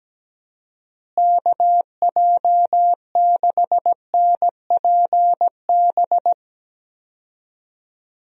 Morse code, heard as a keyed tone, about 17 wpm, 700 Hz. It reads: KJ6NPB
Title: USA Call Signs - 6 Characters - Repeated 3x - 17wpm